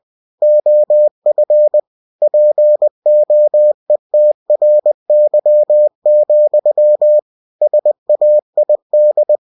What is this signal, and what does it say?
Morse code, heard as a keyed tone, about 20 wpm, 600 Hz. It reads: OF POETRY, SAID